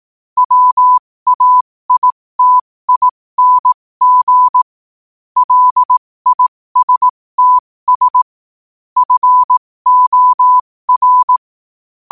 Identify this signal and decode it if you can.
Morse code, heard as a keyed tone, about 18 wpm, spaced as for 15 wpm, 1000 Hz. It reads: WAITING LISTS FOR
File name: qotdfiles / 15wpm